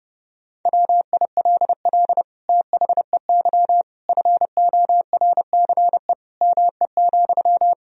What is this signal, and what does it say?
Morse code, heard as a keyed tone, about 32 wpm, 700 Hz. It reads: WILL THEY FORCE ME,
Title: Sentences from Top 1000 Words - Review Part 3 - 30wpm